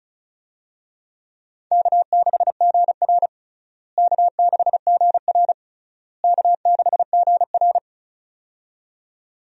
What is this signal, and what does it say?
Morse code, heard as a keyed tone, about 34 wpm, 700 Hz. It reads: K6GR K6GR K6GR
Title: USA Call Signs - 4 Characters - Repeated 3x - 35wpm